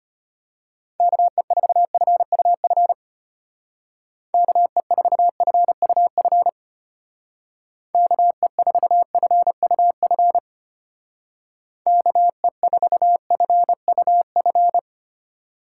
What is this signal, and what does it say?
Morse code, heard as a keyed tone, about 31 wpm, 700 Hz. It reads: KE4FUF KE4FUF KE4FUF KE4FUF
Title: Speed Racing - USA Call Signs 25wpm